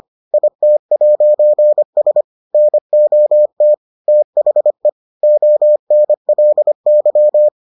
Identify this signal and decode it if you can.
Morse code, heard as a keyed tone, about 25 wpm, 600 Hz. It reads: IT'S NOT THE ONLY